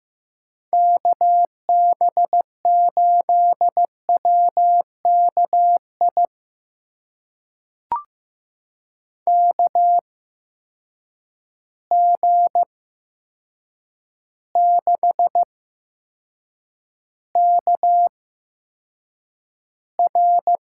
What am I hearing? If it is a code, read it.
Morse code, heard as a keyed tone, about 15 wpm, spaced as for 7 wpm, 700 Hz. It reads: KB8WKI KG6KR
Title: USA Call Signs - 6 Characters - 8x Character Spacing - 15wpm